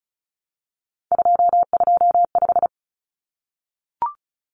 Morse code, heard as a keyed tone, about 34 wpm, 700 Hz. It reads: 225
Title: Common Cut Numbers - 3-Digit 35wpm